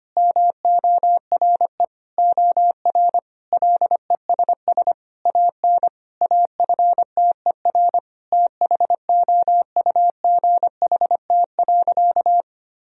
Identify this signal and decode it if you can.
Morse code, heard as a keyed tone, about 25 wpm, 700 Hz. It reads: MORE OR LESS AN AFTER THOUGHT.
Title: Sentences from Top 1000 Words - Review Part 2 - 25wpm